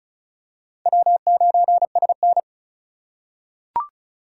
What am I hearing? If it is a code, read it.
Morse code, heard as a keyed tone, about 34 wpm, 700 Hz. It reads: W9SN
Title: USA Call Signs - 4 Characters - 2x Character Spacing - 35wpm